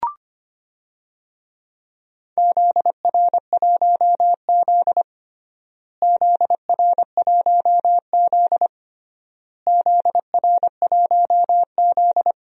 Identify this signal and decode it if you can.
Morse code, heard as a keyed tone, about 25 wpm, 700 Hz. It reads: ZR1Z ZR1Z ZR1Z
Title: DXCC-100 Call Signs - 4 Characters - Repeated 3x - 25wpm